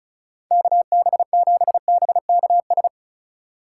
Morse code, heard as a keyed tone, about 36 wpm, 700 Hz. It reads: KB7BKS
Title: USA Call Signs - 6 Characters - 3x Character Spacing - 35wpm